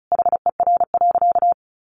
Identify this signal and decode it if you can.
Morse code, heard as a keyed tone, about 34 wpm, 700 Hz. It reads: HER.
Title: Sentences from Top 900 Words 35wpm